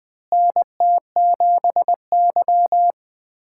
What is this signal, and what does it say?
Morse code, heard as a keyed tone, about 21 wpm, 700 Hz. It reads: NT7Y
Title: USA Callsign DE Callsign - 4 Characters - 20wpm